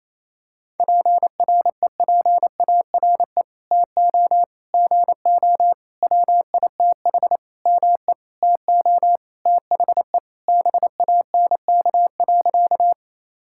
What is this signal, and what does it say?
Morse code, heard as a keyed tone, about 28 wpm, 700 Hz. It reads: PREPARE TO GO WITH ME TO THE BANK.
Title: Sentences from Top 1000 Words - Review Part 6 - 28wpm